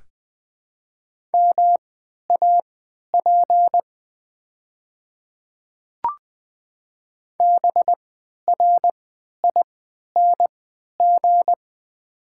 Morse code, heard as a keyed tone, about 20 wpm, spaced as for 11 wpm, 700 Hz. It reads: MAP BRING
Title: 3x Character Spacing - Top 500 Words - 20wpm